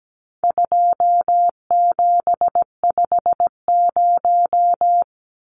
Morse code, heard as a keyed tone, about 17 wpm, 700 Hz. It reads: 2750
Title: Numbers - 4-Digit 17wpm